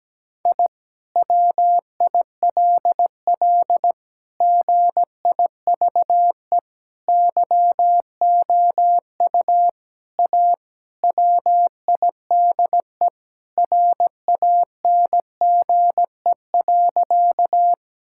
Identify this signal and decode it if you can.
Morse code, heard as a keyed tone, about 17 wpm, 700 Hz. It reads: I WILL GIVE YOU A WIDE RANGE.
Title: Sentences from Top 1000 Words - Review Part 6 - 17wpm